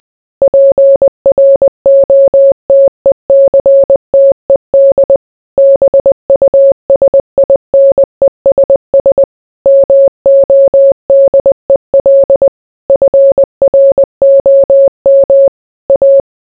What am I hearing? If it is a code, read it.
Morse code, heard as a keyed tone, about 20 wpm, 550 Hz. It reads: PROTECTED BUSINESS MODEL FROM A